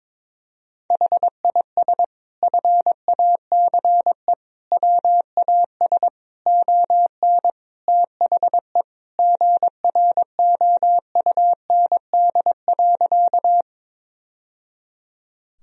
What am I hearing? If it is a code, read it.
Morse code, heard as a keyed tone, about 22 wpm, 700 Hz. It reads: HIS FACE WAS ON THE GROUND.
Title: Sentences from Top 1000 Words - Review Part 2 - 22wpm